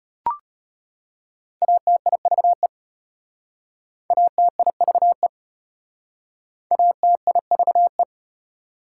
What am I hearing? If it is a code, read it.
Morse code, heard as a keyed tone, about 32 wpm, 700 Hz. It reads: ATIVE ATIVE ATIVE